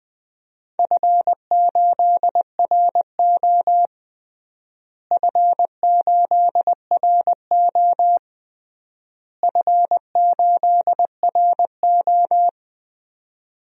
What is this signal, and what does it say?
Morse code, heard as a keyed tone, about 20 wpm, 700 Hz. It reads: F8RO F8RO F8RO
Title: DXCC-100 Call Signs - 4 Characters - Repeated 3x - 20wpm